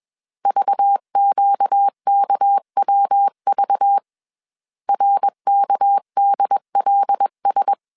Morse code, heard as a keyed tone, about 21 wpm, 800 Hz. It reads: VQKWV RKDLS
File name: CW / DET / 1 6 1500C S A 20wpm 2048